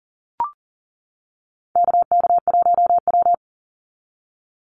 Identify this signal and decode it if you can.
Morse code, heard as a keyed tone, about 40 wpm, 700 Hz. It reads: KK1W
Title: NA SOTA Activator Callsigns 40wpm